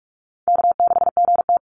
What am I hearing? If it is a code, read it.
Morse code, heard as a keyed tone, about 44 wpm, 700 Hz. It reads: K6GT
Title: Easy Call Signs - 45wpm